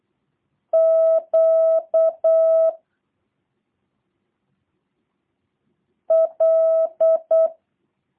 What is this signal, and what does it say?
Morse code, heard as a keyed tone, about 8 wpm, 660 Hz. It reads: Q L
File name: jh8cbh / 2023cw / review1-6